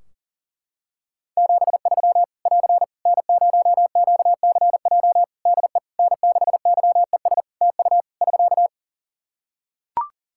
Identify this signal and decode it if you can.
Morse code, heard as a keyed tone, about 40 wpm, 700 Hz. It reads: 73 <AR> N0QCJ DE N6YES TU <SK>
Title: K1USN SST QSO Practice 40wpm